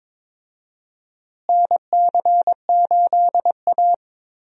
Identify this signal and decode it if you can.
Morse code, heard as a keyed tone, about 22 wpm, 700 Hz. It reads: NC8A
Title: USA Call Signs - 4 Characters - Repeated 3x - 22wpm